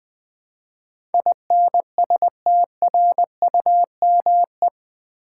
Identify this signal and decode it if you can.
Morse code, heard as a keyed tone, about 20 wpm, 700 Hz. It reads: INSTRUME
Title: Build A Word - 4 Letters At A Time - 12-Letter Words - 20wpm